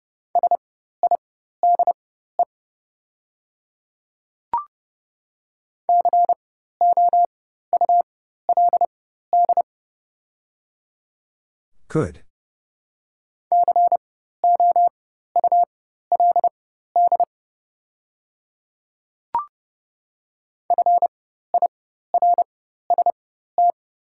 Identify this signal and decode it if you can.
Morse code, heard as a keyed tone, about 30 wpm, spaced as for 14 wpm, 700 Hz. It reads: SIDE COULD COULD FIRST
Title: 4x Character Spacing - Top 200 Words - 30wpm